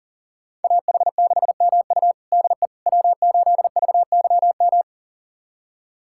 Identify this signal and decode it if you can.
Morse code, heard as a keyed tone, about 40 wpm, 700 Hz. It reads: AH6MU DE W8VYM
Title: USA Callsign DE Callsign - 5 Characters - 40wpm